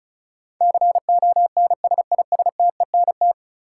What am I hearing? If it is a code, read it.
Morse code, heard as a keyed tone, about 36 wpm, 700 Hz. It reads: CONSISTENT
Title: Sentences from Top 1500 Words - New Word Drill Alt - 35wpm